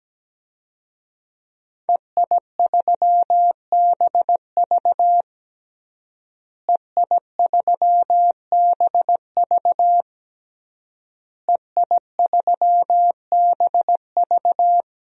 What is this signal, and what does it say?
Morse code, heard as a keyed tone, about 17 wpm, 700 Hz. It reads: EI3BV EI3BV EI3BV
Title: DXCC-100 Call Signs - 5 Characters - Repeated 3x - 17wpm